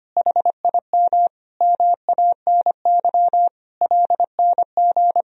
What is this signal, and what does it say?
Morse code, heard as a keyed tone, about 25 wpm, 700 Hz. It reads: HIM MANY LNG
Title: Sentences from Top 200 Words and Abbreviations 25wpm